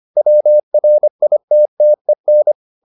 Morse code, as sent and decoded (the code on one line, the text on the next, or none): .-- .-. .. - - . -.
WRITTEN